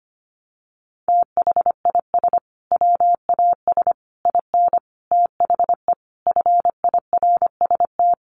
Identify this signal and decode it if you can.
Morse code, heard as a keyed tone, about 25 wpm, 700 Hz. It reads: THIS WAS IN THE FIRST
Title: Sentences from Top 1300 Words 25wpm